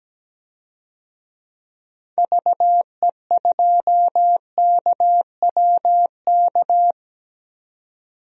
Morse code, heard as a keyed tone, about 17 wpm, 700 Hz. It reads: VE2KWK
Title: DXCC-100 Call Signs - 6 Characters - 17wpm